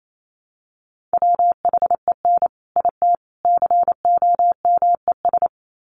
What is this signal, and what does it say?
Morse code, heard as a keyed tone, about 28 wpm, 700 Hz. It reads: WHEN IT COMES
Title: Sets of 3 Words 28wpm